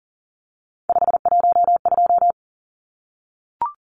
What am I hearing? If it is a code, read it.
Morse code, heard as a keyed tone, about 40 wpm, 700 Hz. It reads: 512
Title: Numbers - 3-Digit 40wpm